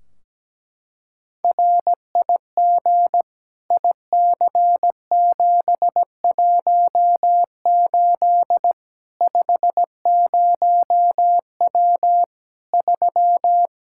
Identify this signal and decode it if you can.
Morse code, heard as a keyed tone, about 17 wpm, 700 Hz. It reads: RIG IC718 50W 3
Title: Practice QSO - Exchange 2 - Info - 17wpm